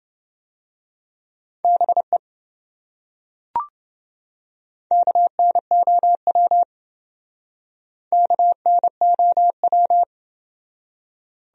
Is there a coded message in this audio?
Yes: Morse code from 1.6 to 10.0 s, heard as a keyed tone, about 28 wpm, 700 Hz. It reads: BE KNOW KNOW